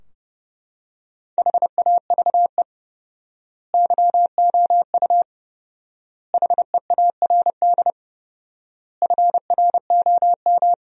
Morse code, heard as a keyed tone, about 32 wpm, 700 Hz. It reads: HAVE YOU HEARD FROM
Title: 4x Word-Spacing - Sentences from Top 400 Words - 30wpm